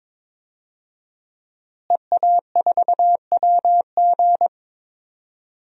Morse code, heard as a keyed tone, about 22 wpm, 700 Hz. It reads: EA4WG
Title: DXCC-100 Call Signs - 5 Characters - Repeated 3x - 22wpm